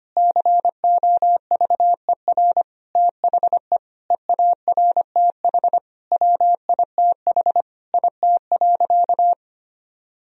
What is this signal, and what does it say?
Morse code, heard as a keyed tone, about 25 wpm, 700 Hz. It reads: COVER THE EARTH WITH IT.
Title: Sentences from Top 1000 Words - Review Part 1 - 25wpm